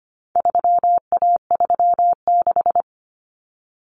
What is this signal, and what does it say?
Morse code, heard as a keyed tone, about 25 wpm, 700 Hz. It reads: 3A36
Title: Common Cut Numbers - 4-Digit 25wpm